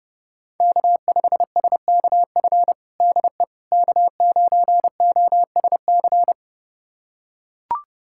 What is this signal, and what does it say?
Morse code, heard as a keyed tone, about 30 wpm, 700 Hz. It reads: K5SKF DE K9OSC
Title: USA Callsign DE Callsign - 5 Characters - 30wpm